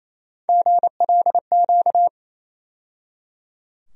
Morse code, heard as a keyed tone, about 28 wpm, 700 Hz. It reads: GLQ
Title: USA Call Sign Postfixes - 3 Letters 28wpm